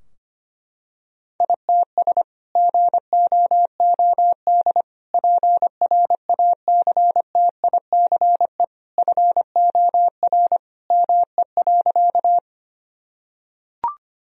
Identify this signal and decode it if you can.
Morse code, heard as a keyed tone, about 25 wpm, 700 Hz. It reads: ITS GOOD PRACTICE FOR ME.
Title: Sentences from Top 1000 Words - Review Part 4 - 25wpm